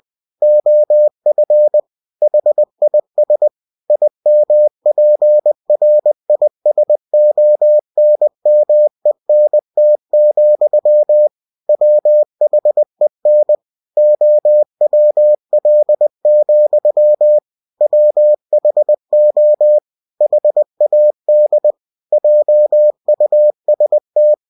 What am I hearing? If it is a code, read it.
Morse code, heard as a keyed tone, about 20 wpm, 600 Hz. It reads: OF HIS IMPRISONMENT, WHEN OWL, WHO HAD JUST
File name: cw-books / winnie-the-pooh / winnie-20wpm-page-0100